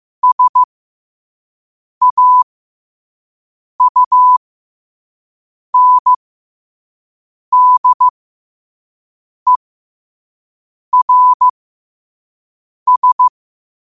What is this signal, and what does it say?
Morse code, heard as a keyed tone, about 15 wpm, spaced as for 5 wpm, 1000 Hz. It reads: SAUNDERS
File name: qotdfiles / 5wpm